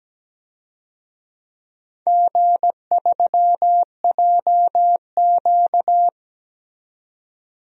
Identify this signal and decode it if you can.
Morse code, heard as a keyed tone, about 17 wpm, 700 Hz. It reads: G3JQ
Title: DXCC-100 Call Signs - 4 Characters - 17wpm